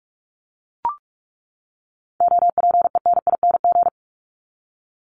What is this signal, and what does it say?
Morse code, heard as a keyed tone, about 44 wpm, 700 Hz. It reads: OPENING
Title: ING Suffix - Top 100 Words with Base Words 45wpm